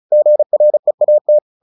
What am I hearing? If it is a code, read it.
Morse code, heard as a keyed tone, about 34 wpm, 600 Hz. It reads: GREAT
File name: the old man and the sea / the old man and the sea 1 35wpm